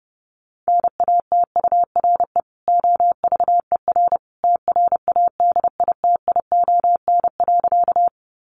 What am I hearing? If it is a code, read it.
Morse code, heard as a keyed tone, about 30 wpm, 700 Hz. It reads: NATURE OVER TRADITION.
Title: Sentences from Top 1500 Words 30wpm